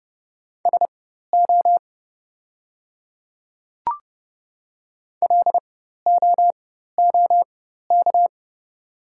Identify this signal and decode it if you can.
Morse code, heard as a keyed tone, about 30 wpm, spaced as for 14 wpm, 700 Hz. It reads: SO LOOK
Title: 4x Character Spacing - Top 200 Words - 30wpm